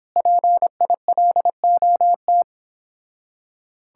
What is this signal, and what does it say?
Morse code, heard as a keyed tone, about 29 wpm, 700 Hz. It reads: PILOT